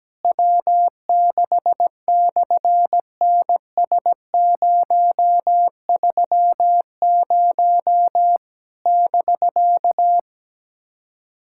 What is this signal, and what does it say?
Morse code, heard as a keyed tone, about 17 wpm, 700 Hz. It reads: W6/NS030 <BK>
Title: SOTA QSO Practice 17wpm